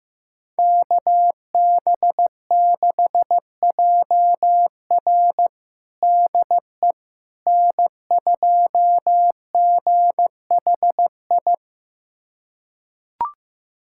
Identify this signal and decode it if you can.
Morse code, heard as a keyed tone, about 15 wpm, 700 Hz. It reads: KB6JR DE N2GHI
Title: USA Callsign DE Callsign 15wpm